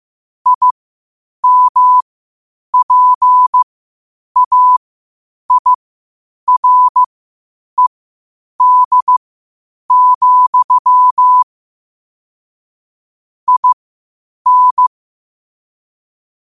Morse code, heard as a keyed tone, about 15 wpm, spaced as for 8 wpm, 1000 Hz. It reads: IMPAIRED, IN
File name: qotdfiles / 8wpm